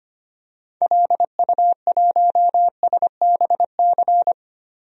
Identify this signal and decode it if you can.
Morse code, heard as a keyed tone, about 25 wpm, 700 Hz. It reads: LU1SBC
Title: DXCC-100 Call Signs - 6 Characters - 25wpm